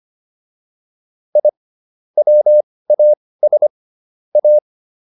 Morse code, heard as a keyed tone, about 25 wpm, spaced as for 18 wpm, 600 Hz. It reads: I WAS A